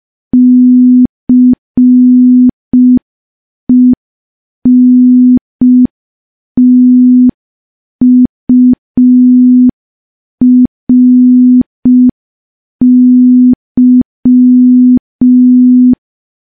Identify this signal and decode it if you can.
Morse code, heard as a keyed tone, about 5 wpm, 250 Hz. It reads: CENTURY
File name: morse / morse-training-05WPM